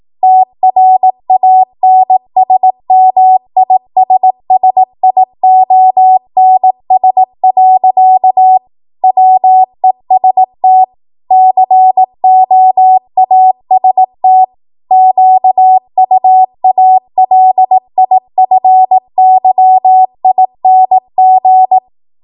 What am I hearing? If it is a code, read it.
Morse code, heard as a keyed tone, about 18 wpm, 750 Hz. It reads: TRANSMISSIONS. WEST COAST QUALIFYING
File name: Archive / 18 WPM / 18WPM